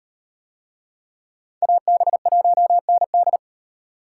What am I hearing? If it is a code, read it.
Morse code, heard as a keyed tone, about 36 wpm, 700 Hz. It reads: AB1ND